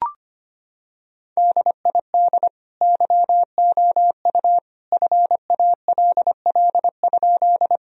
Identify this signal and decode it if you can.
Morse code, heard as a keyed tone, about 25 wpm, 700 Hz. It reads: DID YOU FALL?
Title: Sentences from Top 1000 Words - Review Part 2 - 25wpm